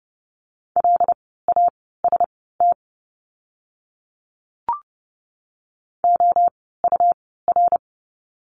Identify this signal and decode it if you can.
Morse code, heard as a keyed tone, about 30 wpm, spaced as for 17 wpm, 700 Hz. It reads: LAST OUR
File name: Lesson 17f - Letter u - Words 30wpm